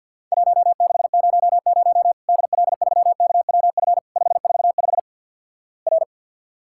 Morse code, heard as a keyed tone, about 50 wpm, 700 Hz. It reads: 1600 DL3KWF 545 R
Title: Worked All Europe DX Contest - QTC Exchange with Repeats - 50wpm